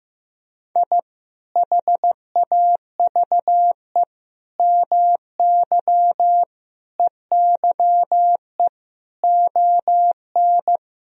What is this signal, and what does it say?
Morse code, heard as a keyed tone, about 15 wpm, 700 Hz. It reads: I HAVE MY EYE ON
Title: Sentences from Top 1000 Words - Review Part 1 - 15wpm